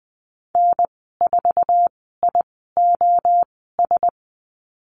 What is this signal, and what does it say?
Morse code, heard as a keyed tone, about 20 wpm, spaced as for 14 wpm, 700 Hz. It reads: N4IOS
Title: 062 Lesson 11g - Letter h - Call Signs 20wpm